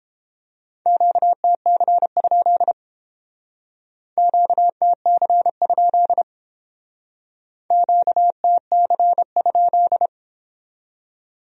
Morse code, heard as a keyed tone, about 30 wpm, 700 Hz. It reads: QTC? QTC? QTC?